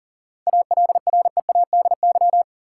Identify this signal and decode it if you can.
Morse code, heard as a keyed tone, about 38 wpm, 700 Hz. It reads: ALREADY
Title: Seven Letter Words - 40wpm